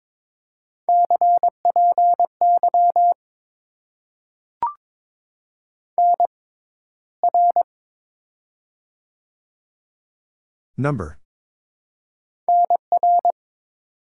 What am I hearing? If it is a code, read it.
Morse code, heard as a keyed tone, about 22 wpm, 700 Hz. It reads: C P Y N R N R
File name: Top 90 QSO Elements - 6x Character Spacing - 22wpm